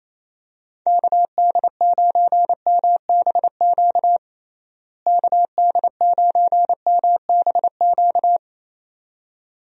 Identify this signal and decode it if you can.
Morse code, heard as a keyed tone, about 27 wpm, 700 Hz. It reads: KD9MBQ KD9MBQ
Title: USA Call Signs - 6 Characters - Repeated 2x - 28wpm